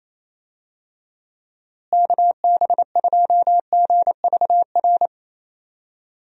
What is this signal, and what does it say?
Morse code, heard as a keyed tone, about 28 wpm, 700 Hz. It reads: KB2GVR
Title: USA Call Signs - 6 Characters - Repeated 3x - 28wpm